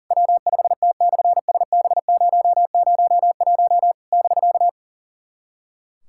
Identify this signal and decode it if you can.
Morse code, heard as a keyed tone, about 40 wpm, 700 Hz. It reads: W5T/SB001 <BK>
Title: SOTA QSO Practice 40wpm